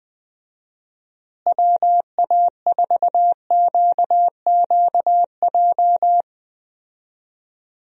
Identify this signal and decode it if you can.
Morse code, heard as a keyed tone, about 20 wpm, 700 Hz. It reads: WA4QQJ
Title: USA Call Signs - 6 Characters - Repeated 2x - 20wpm